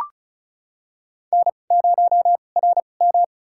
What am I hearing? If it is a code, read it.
Morse code, heard as a keyed tone, about 34 wpm, spaced as for 25 wpm, 700 Hz. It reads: N0RM
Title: USA Call Signs - 2x Character Spacing - 35wpm